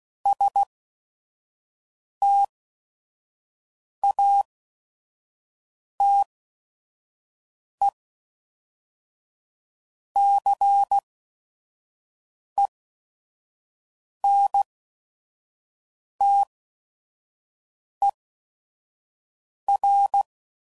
Morse code, heard as a keyed tone, about 16 wpm, spaced as for 5 wpm, 800 Hz. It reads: STATECENTER
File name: CW Code Groups @5WPM / fromSM's / Cities